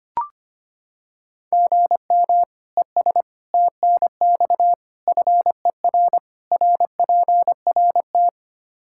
Morse code, heard as a keyed tone, about 25 wpm, 700 Hz. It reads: GM ES TNX FER RPRT
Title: Practice QSO - Exchange 1 - Beginning - 25wpm